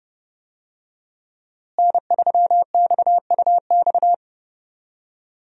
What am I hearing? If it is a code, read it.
Morse code, heard as a keyed tone, about 30 wpm, 700 Hz. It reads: N3XUX
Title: USA Call Signs - 5 Characters - Repeated 2x - 30wpm